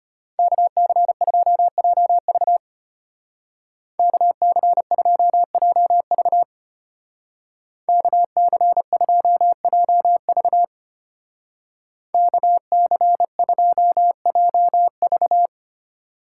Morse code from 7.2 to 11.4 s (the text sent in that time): KC2JV